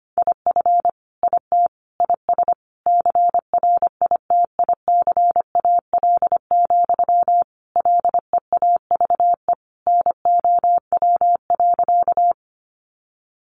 Morse code, heard as a keyed tone, about 25 wpm, 700 Hz. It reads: IF IT IS CRITICAL, LEAVE NOW.
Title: Sentences from Top 1100 Words 25wpm